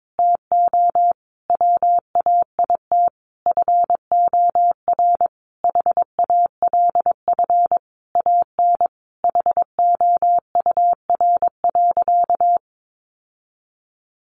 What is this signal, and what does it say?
Morse code, heard as a keyed tone, about 22 wpm, 700 Hz. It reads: TO WAIT FOR HALF AN HOUR.
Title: Sentences from Top 1100 Words 22wpm